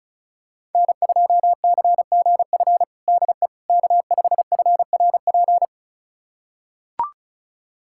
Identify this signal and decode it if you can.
Morse code, heard as a keyed tone, about 34 wpm, 700 Hz. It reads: N2CGF DE K5FRP
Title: USA Callsign DE Callsign - 5 Characters - 35wpm